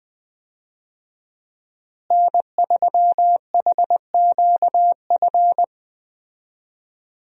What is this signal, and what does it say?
Morse code, heard as a keyed tone, about 20 wpm, 700 Hz. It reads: N3HQF
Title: USA Call Signs - 5 Characters - Repeated 2x - 20wpm